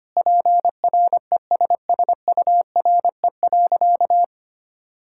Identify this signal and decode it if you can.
Morse code, heard as a keyed tone, about 25 wpm, 700 Hz. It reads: PRESSURE.
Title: Sentences from Top 1000 Words - Review Part 6 - 25wpm